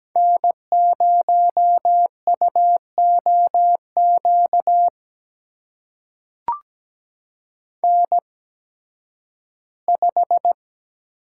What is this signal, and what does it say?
Morse code, heard as a keyed tone, about 17 wpm, spaced as for 11 wpm, 700 Hz. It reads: N0UOQ N5
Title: USA Call Signs - 8x Character Spacing - 17wpm